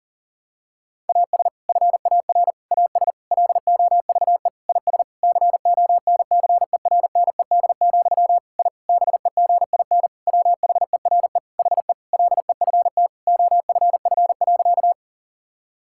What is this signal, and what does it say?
Morse code, heard as a keyed tone, about 37 wpm, 700 Hz. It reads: AS FAR AS LOVE IS CONCERNED, I BEGIN WHERE HE LEFT OFF.
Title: Sentences from Top 1000 Words - Review Part 6 - 40wpm